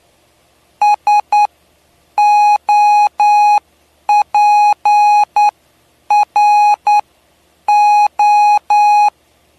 Morse code, heard as a keyed tone, about 9 wpm, spaced as for 8 wpm, 830 Hz. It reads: SOPRO
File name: cw / py2lcd / 07-PY2LCD-Li��o 7